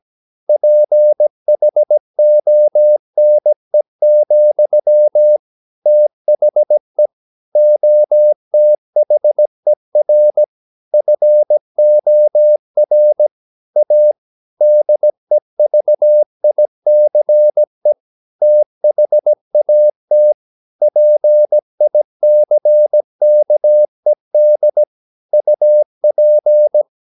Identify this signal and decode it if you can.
Morse code, heard as a keyed tone, about 17 wpm, 600 Hz. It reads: PHONE, THE OTHER FOR A DEVICE THAT PICKED UP
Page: Romanen Never af Ken Follet på engelsk.